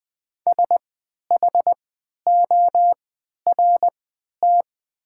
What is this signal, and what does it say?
Morse code, heard as a keyed tone, about 20 wpm, spaced as for 11 wpm, 700 Hz. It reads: SHORT
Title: 3x Character Spacing - Top 500 Words - 20wpm